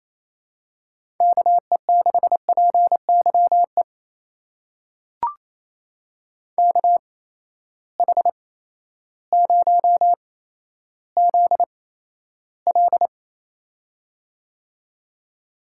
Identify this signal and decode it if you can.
Morse code, heard as a keyed tone, about 28 wpm, spaced as for 13 wpm, 700 Hz. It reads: KE6PYE KH0ZL
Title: USA Call Signs - 8x Character Spacing - 28wpm